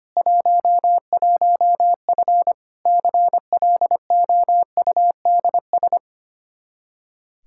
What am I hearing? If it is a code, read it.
Morse code, heard as a keyed tone, about 25 wpm, 700 Hz. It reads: 11F CLOUDS
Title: Practice QSO - Exchange 2 - Info - 25wpm